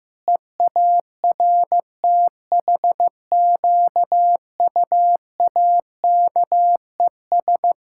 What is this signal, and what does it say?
Morse code, heard as a keyed tone, about 15 wpm, 700 Hz. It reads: EARTHQUAKES
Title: Pangram Challenge v2 - 15wpm